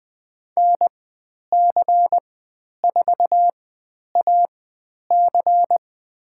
Morse code, heard as a keyed tone, about 20 wpm, spaced as for 10 wpm, 700 Hz. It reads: NC4AC
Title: 109 Lesson 18g - Letter m - Call Signs 20wpm